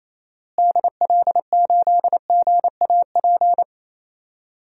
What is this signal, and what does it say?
Morse code, heard as a keyed tone, about 28 wpm, 700 Hz. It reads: DL8GAP
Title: DXCC-100 Call Signs - 6 Characters - Repeated 2x - 28wpm